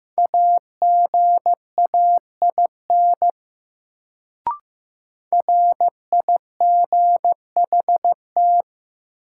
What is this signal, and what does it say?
Morse code, heard as a keyed tone, about 15 wpm, 700 Hz. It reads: AGAIN RIGHT
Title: Sentences from Top 200 Words - New Word Drill - 15wpm